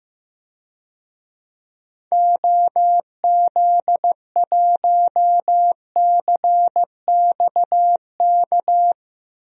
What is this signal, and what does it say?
Morse code, heard as a keyed tone, about 15 wpm, 700 Hz. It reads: OZ1CXK
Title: DXCC-100 Call Signs - 6 Characters - Repeated 3x - 15wpm